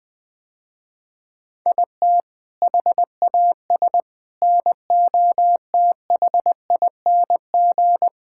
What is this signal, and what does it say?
Morse code, heard as a keyed tone, about 21 wpm, 700 Hz. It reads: IT HAS NOTHING
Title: Sentences with 4-Word Phrases - Spoken Beginning - 20wpm